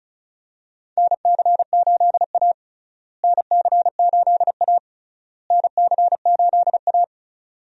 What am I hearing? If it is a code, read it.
Morse code, heard as a keyed tone, about 34 wpm, 700 Hz. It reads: NC8A NC8A NC8A
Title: USA Call Signs - 4 Characters - Repeated 3x - 35wpm